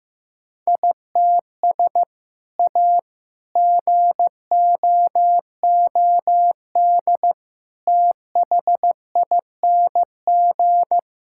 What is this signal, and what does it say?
Morse code, heard as a keyed tone, about 15 wpm, 700 Hz. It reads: ITS A GOOD THING
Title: Sentences from Top 1000 Words - Review Part 5 - 15wpm